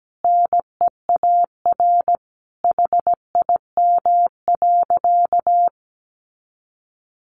Morse code, heard as a keyed tone, about 17 wpm, 700 Hz. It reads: NEAR HIM.
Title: Sentences from Top 200 Words 17wpm